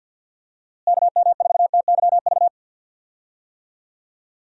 Morse code, heard as a keyed tone, about 48 wpm, 700 Hz. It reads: KM4TYV
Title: USA Call Signs - 6 Characters - Repeated 3x - 50wpm